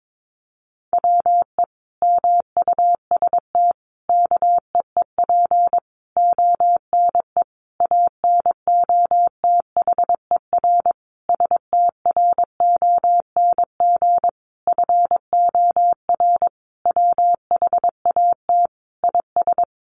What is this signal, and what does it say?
Morse code, heard as a keyed tone, about 22 wpm, 700 Hz. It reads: WE MUST KEEP ONE ANOTHER STRONG FOR WHAT IS
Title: Sentences from Top 600 Words 22wpm